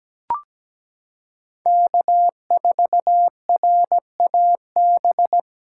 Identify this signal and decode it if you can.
Morse code, heard as a keyed tone, about 17 wpm, 700 Hz. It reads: K4RAB
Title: K1USN SST QSO Practice - 17wpm